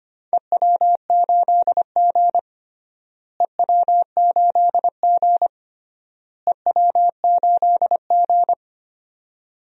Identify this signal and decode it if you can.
Morse code, heard as a keyed tone, about 25 wpm, 700 Hz. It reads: EW8G EW8G EW8G
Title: DXCC-100 Call Signs - 4 Characters - Repeated 3x - 25wpm